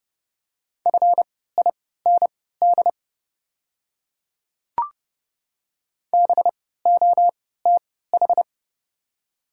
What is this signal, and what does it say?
Morse code, heard as a keyed tone, about 30 wpm, spaced as for 17 wpm, 700 Hz. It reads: FIND BOTH
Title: 3x Character Spacing - Top 500 Words - 30wpm